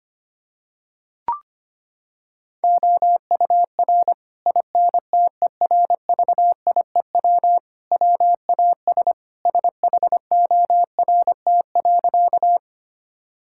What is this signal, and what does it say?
Morse code, heard as a keyed tone, about 25 wpm, 700 Hz. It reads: OUR INTERVIEW WAS SHORT.
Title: Sentences from Top 1600 Words 25wpm